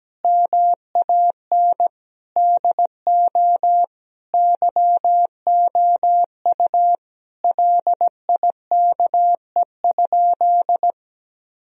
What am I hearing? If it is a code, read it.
Morse code, heard as a keyed tone, about 17 wpm, 700 Hz. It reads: MAN DO YOU LIKE?
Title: Sentences from Top 1000 Words - Review Part 3 - 17wpm